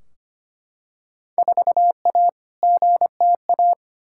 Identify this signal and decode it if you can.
Morse code, heard as a keyed tone, about 25 wpm, 700 Hz. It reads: 4A GTA
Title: Field Day Exchanges Encore - 25wpm